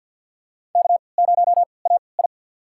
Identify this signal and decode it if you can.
Morse code, heard as a keyed tone, about 46 wpm, spaced as for 27 wpm, 700 Hz. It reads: K0AI
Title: USA Call Signs - 4 Characters - 3x Character Spacing - 50wpm